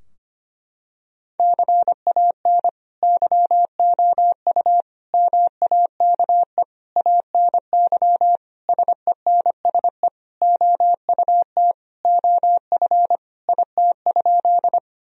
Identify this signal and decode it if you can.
Morse code, heard as a keyed tone, about 25 wpm, 700 Hz. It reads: CAN YOU MAKE ANY SENSE OUT OF IT?
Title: Sentences from Top 1000 Words - Review Part 4 - 25wpm